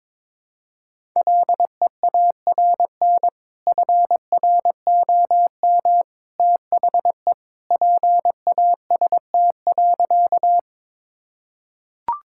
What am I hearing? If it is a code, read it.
Morse code, heard as a keyed tone, about 22 wpm, 700 Hz. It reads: LEARN FROM THE PAST.
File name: practice / 22wpm / Sentences with 2-Word Phrases - 22wpm